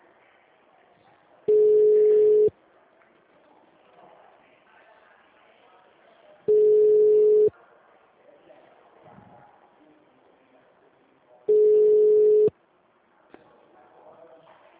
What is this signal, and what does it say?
Ringback tone (Europe, 425 Hz).